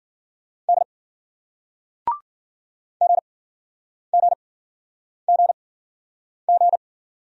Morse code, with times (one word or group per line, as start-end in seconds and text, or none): 0.7-0.8 D
3.0-6.8 GGGG